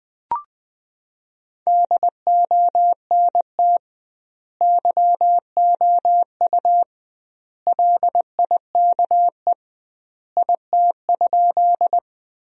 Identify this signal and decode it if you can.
Morse code, heard as a keyed tone, about 20 wpm, 700 Hz. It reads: DONT YOU LIKE IT?
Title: 2x Word-Spacing - Sentences from Top 300 Words - 20wpm